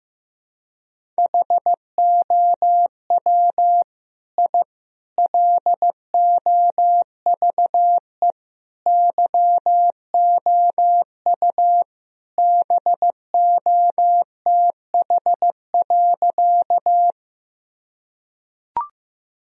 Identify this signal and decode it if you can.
Morse code, heard as a keyed tone, about 15 wpm, 700 Hz. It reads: HOW I LOVE YOU BOTH.
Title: Sentences from Top 1000 Words - Review Part 1 - 15wpm